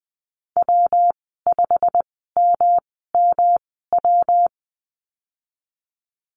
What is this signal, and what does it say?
Morse code, heard as a keyed tone, about 20 wpm, spaced as for 15 wpm, 700 Hz. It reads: W5MMW
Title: 161 Lesson 26g - Letter g - Call Signs 20wpm